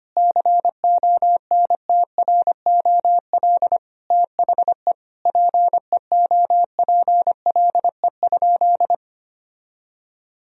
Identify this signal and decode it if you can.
Morse code, heard as a keyed tone, about 25 wpm, 700 Hz. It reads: CONTROL THE PEOPLE?
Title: Sentences from Top 1000 Words - Review Part 3 - 25wpm